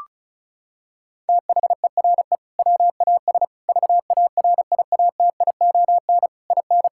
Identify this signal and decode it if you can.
Morse code, heard as a keyed tone, about 36 wpm, 700 Hz. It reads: THERE WAS VARIATION IN
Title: Sentences from Top 1600 Words 35wpm